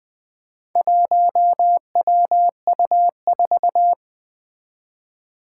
Morse code, heard as a keyed tone, about 20 wpm, 700 Hz. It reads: 1WU4
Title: Four Challenging Letter-Number Combinations 20wpm